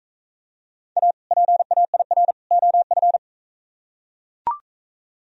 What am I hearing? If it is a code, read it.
Morse code, heard as a keyed tone, about 44 wpm, 700 Hz. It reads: A PAIR OF